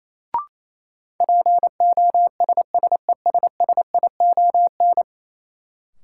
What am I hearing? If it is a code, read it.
Morse code, heard as a keyed tone, about 28 wpm, 700 Hz. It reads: POSSESSION
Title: Sentences from Top 1400 Words - New Word Drill Alt - 28wpm